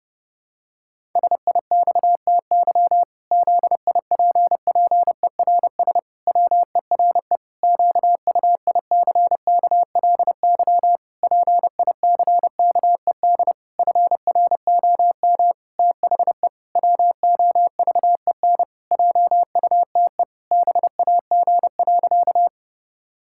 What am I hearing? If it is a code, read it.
Morse code, heard as a keyed tone, about 30 wpm, 700 Hz. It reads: SIXTY ZIPPERS WERE QUICKLY PICKED FROM THE WOVEN JUTE BAG.
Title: Pangram Challenge v2 - 30wpm